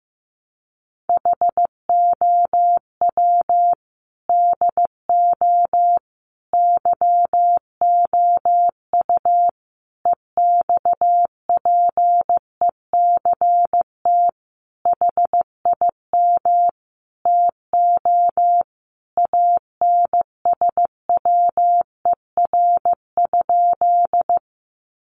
Morse code, heard as a keyed tone, about 15 wpm, 700 Hz. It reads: HOW DO YOU EXPECT HIM TO ANSWER?
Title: Sentences from Top 1100 Words 15wpm